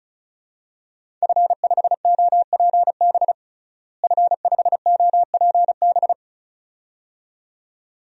Morse code, heard as a keyed tone, about 36 wpm, 700 Hz. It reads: F5OPB F5OPB
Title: DXCC-100 Call Signs - 5 Characters - Repeated 2x - 35wpm